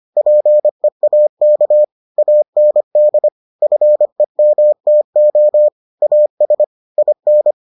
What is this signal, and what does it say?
Morse code, heard as a keyed tone, about 25 wpm, 600 Hz. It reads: PEAK AND FEMTO AS IN